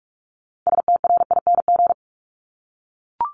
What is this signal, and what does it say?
Morse code, heard as a keyed tone, about 44 wpm, 700 Hz. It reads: STRING
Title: Top 1501-1750 Words 45wpm